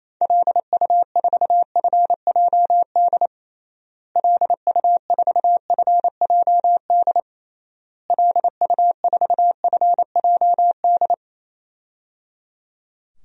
Morse code, heard as a keyed tone, about 28 wpm, 700 Hz. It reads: LU4FJD LU4FJD LU4FJD
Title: DXCC-100 Call Signs - 6 Characters - Repeated 3x - 28wpm